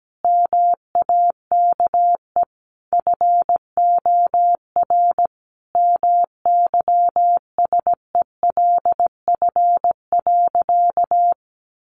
Morse code, heard as a keyed tone, about 17 wpm, 700 Hz. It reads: MAKE FOR MYSELF.
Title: Sentences from Top 1300 Words 17wpm